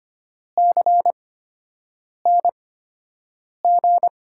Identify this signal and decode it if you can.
Morse code, heard as a keyed tone, about 25 wpm, spaced as for 7 wpm, 700 Hz. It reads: CNG
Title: USA Call Signs - 6 Characters - 8x Character Spacing - 25wpm